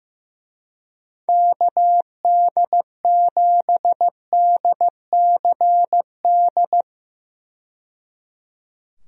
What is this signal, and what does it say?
Morse code, heard as a keyed tone, about 15 wpm, 700 Hz. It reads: KD7DCD
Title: USA Call Signs - 6 Characters - 15wpm